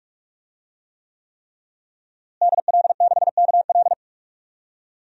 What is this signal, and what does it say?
Morse code, heard as a keyed tone, about 45 wpm, 700 Hz. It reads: DL6KL
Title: DXCC-100 Call Signs - 5 Characters - Repeated 3x - 45wpm